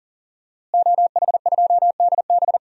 Morse code, heard as a keyed tone, about 40 wpm, 700 Hz. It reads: OH2DB
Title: DXCC-100 Call Signs - 5 Characters - Repeated 2x - 40wpm